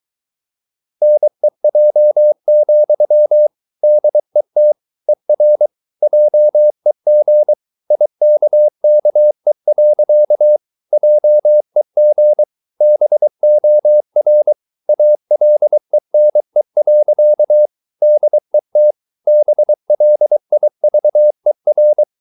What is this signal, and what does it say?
Morse code, heard as a keyed tone, about 23 wpm, 600 Hz. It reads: NEJ, DET ER JEG IKKE. JEG BOR ALENE. DET BLIVER
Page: Ferie 23 wpm